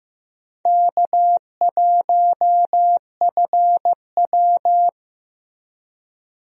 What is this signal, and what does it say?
Morse code, heard as a keyed tone, about 15 wpm, 700 Hz. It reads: K1FW
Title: USA Call Signs - 4 Characters - Repeated 3x - 15wpm